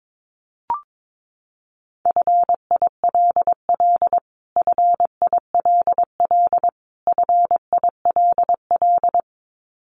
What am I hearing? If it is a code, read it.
Morse code, heard as a keyed tone, about 22 wpm, 700 Hz. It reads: FILL FILL FILL
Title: Top 500 Words Repeated 22wpm